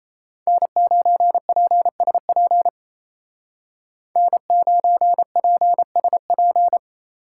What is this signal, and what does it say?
Morse code, heard as a keyed tone, about 33 wpm, 700 Hz. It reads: N9PSP N9PSP
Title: Speed Racing - USA 1x3 Call Signs 28wpm